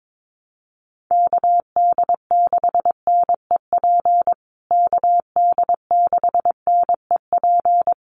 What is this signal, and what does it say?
Morse code, heard as a keyed tone, about 22 wpm, 700 Hz. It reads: KD6NEP KD6NEP
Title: USA Call Signs Repeated 3x 22wpm